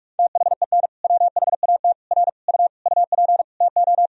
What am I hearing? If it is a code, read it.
Morse code, heard as a keyed tone, about 41 wpm, 700 Hz. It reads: THEN WHAT R U UP TO
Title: Sentences from Top 100 Words and Abbreviations 45wpm